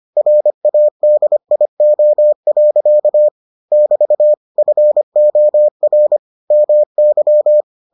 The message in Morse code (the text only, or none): RADIO. <BT> FOR MY